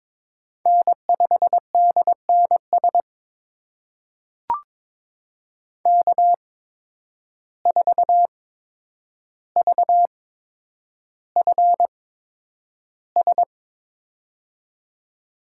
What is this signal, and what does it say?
Morse code, heard as a keyed tone, about 22 wpm, spaced as for 9 wpm, 700 Hz. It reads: N5DNS K4VFS
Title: USA Call Signs - 5 Characters - 8x Character Spacing - 22wpm